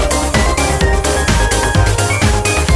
There is no code.